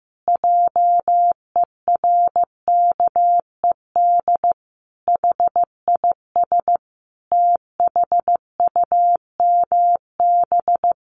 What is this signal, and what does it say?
Morse code, heard as a keyed tone, about 15 wpm, 700 Hz. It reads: JERKED HIS THUMB
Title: A Sound Of Thunder 15wpm